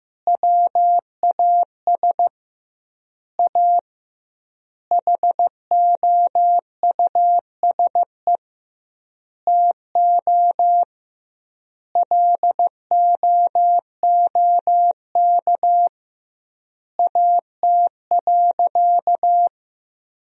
Morse code, heard as a keyed tone, about 15 wpm, 700 Hz. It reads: WAS A HOUSE TO LOOK AT.
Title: 2x Word-Spacing - Sentences from Top 200 Words - 15wpm